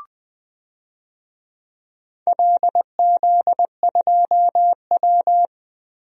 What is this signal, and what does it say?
Morse code, heard as a keyed tone, about 20 wpm, 700 Hz. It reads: LZ2W
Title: DXCC-100 Call Signs - 4 Characters - Repeated 2x - 20wpm